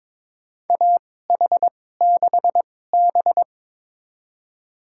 Morse code, heard as a keyed tone, about 22 wpm, spaced as for 16 wpm, 700 Hz. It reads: AH6B
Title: USA Call Signs - 4 Characters - 2x Character Spacing - 22wpm